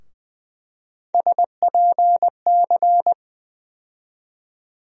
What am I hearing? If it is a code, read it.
Morse code, heard as a keyed tone, about 20 wpm, 700 Hz. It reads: SPC
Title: USA Call Sign Postfixes - 3 Letters 20wpm